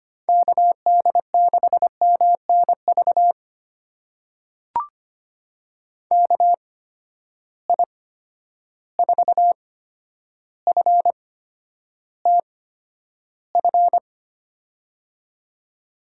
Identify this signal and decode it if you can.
Morse code, heard as a keyed tone, about 25 wpm, spaced as for 11 wpm, 700 Hz. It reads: KD6MNV KI4FTF
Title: USA Call Signs - 6 Characters - 8x Character Spacing - 25wpm